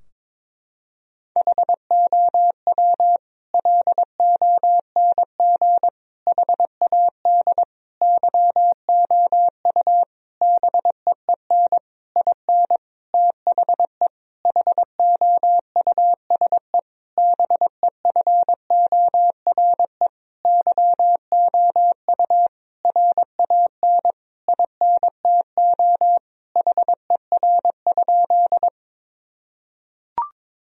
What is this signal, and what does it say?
Morse code, heard as a keyed tone, about 22 wpm, 700 Hz. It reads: HOW LONG HAD YOU BEEN IN THE HOUSE BEFORE YOU RAN INTO HER?
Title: Sentences from Top 1000 Words - Review Part 3 - 22wpm